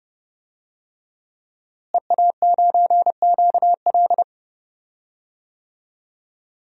Morse code, heard as a keyed tone, about 30 wpm, 700 Hz. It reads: EA9QL
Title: DXCC-100 Call Signs - 5 Characters - Repeated 3x - 30wpm